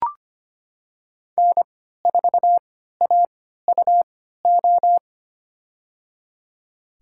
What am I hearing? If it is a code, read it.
Morse code, heard as a keyed tone, about 25 wpm, spaced as for 14 wpm, 700 Hz. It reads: N4AUO
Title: USA Call Signs - 5 Characters - 3x Character Spacing - 25wpm